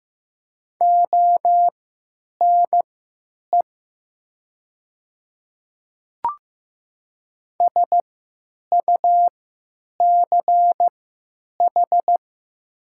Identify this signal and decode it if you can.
Morse code, heard as a keyed tone, about 15 wpm, spaced as for 9 wpm, 700 Hz. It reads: ONE SUCH
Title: 3x Character Spacing - Top 200 Words - 15wpm